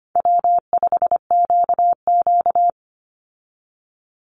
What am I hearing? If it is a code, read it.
Morse code, heard as a keyed tone, about 25 wpm, 700 Hz. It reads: W5QQ
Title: Easy Call Signs Repeated 3x - 25wpm